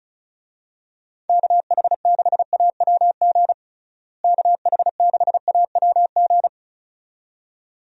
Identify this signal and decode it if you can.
Morse code, heard as a keyed tone, about 35 wpm, 700 Hz. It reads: KH6AWG KH6AWG
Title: DXCC-100 Call Signs - 6 Characters - Repeated 2x - 35wpm